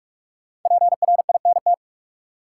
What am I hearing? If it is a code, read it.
Morse code, heard as a keyed tone, about 44 wpm, 700 Hz. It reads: PRINT
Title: Sentences from Top 1400 Words - New Word Drill - 45wpm